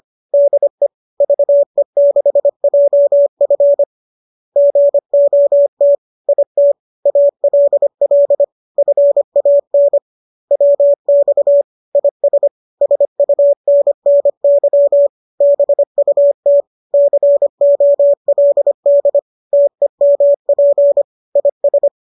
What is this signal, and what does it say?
Morse code, heard as a keyed tone, about 25 wpm, 570 Hz. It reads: DE VE6JF GOT IT ALL FAN WX IS SUNNY BUT COLD TEMP IS
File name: qso203 25